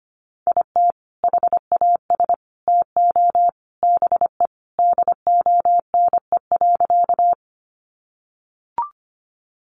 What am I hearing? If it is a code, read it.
Morse code, heard as a keyed tone, about 25 wpm, 700 Hz. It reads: IT HAS TO BE DONE.
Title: Sentences from Top 500 Words 25wpm